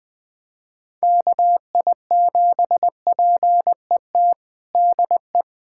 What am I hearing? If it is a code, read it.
Morse code, heard as a keyed tone, about 20 wpm, 700 Hz. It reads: KI7PET DE
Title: USA Callsign DE Callsign - 6 Characters - 20wpm